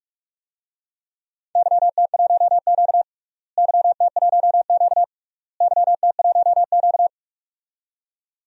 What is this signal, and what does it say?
Morse code, heard as a keyed tone, about 43 wpm, 700 Hz. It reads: YT1Q YT1Q YT1Q
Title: DXCC-100 Call Signs - 4 Characters - Repeated 3x - 45wpm